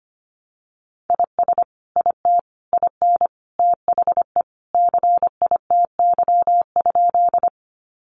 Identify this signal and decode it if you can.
Morse code, heard as a keyed tone, about 25 wpm, 700 Hz. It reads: IS IT IN THE CITY?
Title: Sentences from Top 300 Words 25wpm